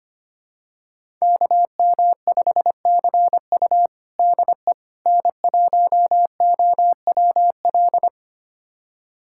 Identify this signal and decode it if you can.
Morse code, heard as a keyed tone, about 25 wpm, 700 Hz. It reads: KM5CU DE N1OWL
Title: USA Callsign DE Callsign - 5 Characters - 25wpm